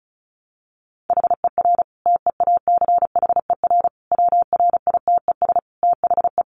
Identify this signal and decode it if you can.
Morse code, heard as a keyed tone, about 34 wpm, 700 Hz. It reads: HER TEACHER WRITES THE
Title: Additional Easy Phrases 35wpm